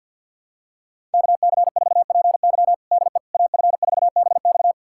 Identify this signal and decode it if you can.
Morse code, heard as a keyed tone, about 53 wpm, 700 Hz. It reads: KC4PY DE AF4BX
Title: USA Callsign DE Callsign - 5 Characters - 50wpm